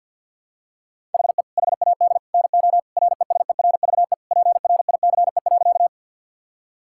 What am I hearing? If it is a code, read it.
Morse code, heard as a keyed tone, about 46 wpm, 700 Hz. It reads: HE HAD NO RESERVE PRICE.
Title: Sentences from Top 1000 Words - Review Part 6 - 50wpm